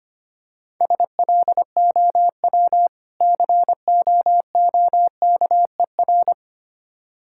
Morse code, heard as a keyed tone, about 27 wpm, 700 Hz. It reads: SLOW COOKER
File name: practice / 25wpm / Difficult Words - Speed-Racing with Hints - 25wpm